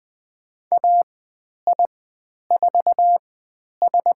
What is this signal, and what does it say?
Morse code, heard as a keyed tone, about 20 wpm, spaced as for 10 wpm, 700 Hz. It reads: AI4S
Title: 102 Lesson 17g - Letter u - Call Signs 20wpm